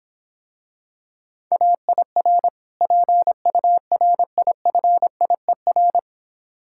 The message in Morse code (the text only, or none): AIR PURIFIER